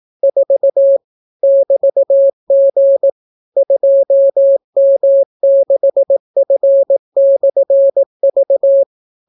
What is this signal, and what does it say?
Morse code, heard as a keyed tone, about 18 wpm, 550 Hz. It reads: 4 <BT>G 2M6F/V